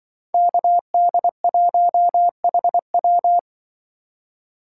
Morse code, heard as a keyed tone, about 24 wpm, 700 Hz. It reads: KD1HW
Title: Speed Racing - USA Call Signs 20wpm